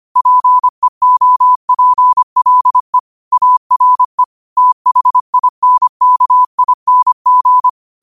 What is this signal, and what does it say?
Morse code, heard as a keyed tone, about 25 wpm, 1000 Hz. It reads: PEOPLE ARE THINKING
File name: qotdfiles / 25wpm